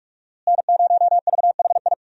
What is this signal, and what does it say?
Morse code, heard as a keyed tone, about 41 wpm, 700 Hz. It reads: N0VHI
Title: USA Call Signs - 5 Characters - Repeated 2x - 45wpm